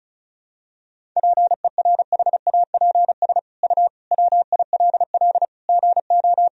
Morse code, heard as a keyed tone, about 36 wpm, 700 Hz. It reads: PERHAPS U WILL GO
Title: Sentences from Top 500 Words and Abbreviations 35wpm